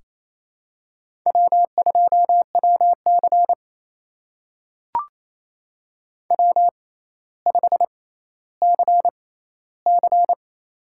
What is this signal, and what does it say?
Morse code, heard as a keyed tone, about 27 wpm, spaced as for 14 wpm, 700 Hz. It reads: W2WC W5CC
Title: USA Call Signs - 4 Characters - 6x Character Spacing - 28wpm